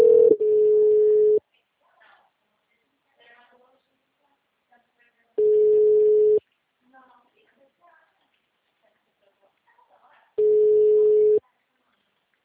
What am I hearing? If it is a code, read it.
Ringback tone (Europe, 425 Hz), with speech.